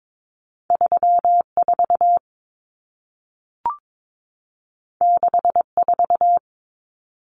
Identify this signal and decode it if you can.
Morse code, heard as a keyed tone, about 22 wpm, 700 Hz. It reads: 34 64
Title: Numbers - 2-Digit 22wpm